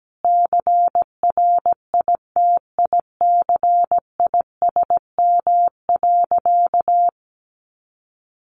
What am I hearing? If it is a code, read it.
Morse code, heard as a keyed tone, about 17 wpm, 700 Hz. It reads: CRITICISM.
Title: Sentences from Top 1300 Words 17wpm